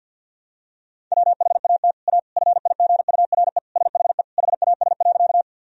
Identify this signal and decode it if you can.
Morse code, heard as a keyed tone, about 50 wpm, 700 Hz. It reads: WHAT A FIGURE SHE HAS.
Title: Sentences from Top 1000 Words - Review Part 2 - 50wpm